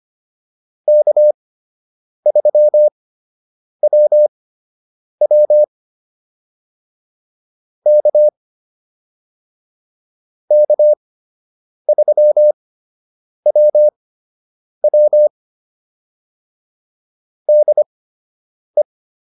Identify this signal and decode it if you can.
Morse code, heard as a keyed tone, about 25 wpm, spaced as for 8 wpm, 600 Hz. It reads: K3WW K K3WW DE